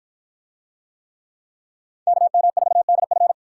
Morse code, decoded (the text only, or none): KM4DF